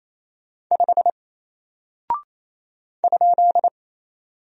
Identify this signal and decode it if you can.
Morse code, heard as a keyed tone, about 27 wpm, 700 Hz. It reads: 5 ?
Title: Single Character (Letter:Number:Punctuation:Prosign) 28wpm